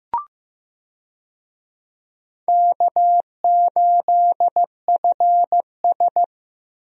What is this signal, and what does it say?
Morse code, heard as a keyed tone, about 15 wpm, 700 Hz. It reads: K8FS
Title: USA Call Signs - 4 Characters - Repeated 3x - 15wpm